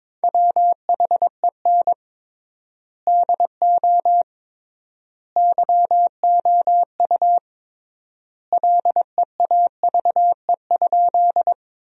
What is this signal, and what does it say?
Morse code, heard as a keyed tone, about 21 wpm, 700 Hz. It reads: WHEN DO YOU LEAVE?
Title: 3x Word-Spacing - Sentences from Top 400 Words - 22wpm